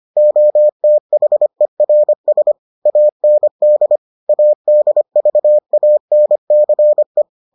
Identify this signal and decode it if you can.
Morse code, heard as a keyed tone, about 25 wpm, 600 Hz. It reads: OTHERS AND ADVANCE